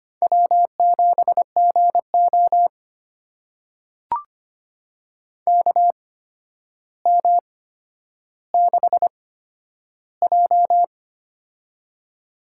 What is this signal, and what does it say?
Morse code, heard as a keyed tone, about 25 wpm, spaced as for 11 wpm, 700 Hz. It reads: W7GO KM6J
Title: USA Call Signs - 8x Character Spacing - 25wpm